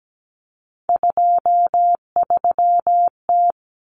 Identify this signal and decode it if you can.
Morse code, heard as a keyed tone, about 17 wpm, 700 Hz. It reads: 23T
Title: Common Cut Numbers - 3-Digit 17wpm